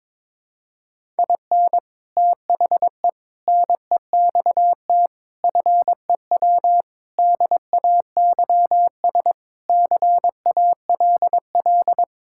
Morse code, heard as a keyed tone, about 22 wpm, 700 Hz. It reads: IN THE NEXT FEW DAYS CALL
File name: practice / 22wpm / Sentences with 3-Word Phrases - 22wpm